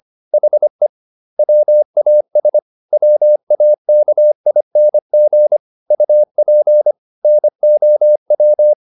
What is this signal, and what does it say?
Morse code, heard as a keyed tone, about 25 wpm, 600 Hz. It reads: HE WAS WAKING UP NOW
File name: the old man and the sea part 1 25wpm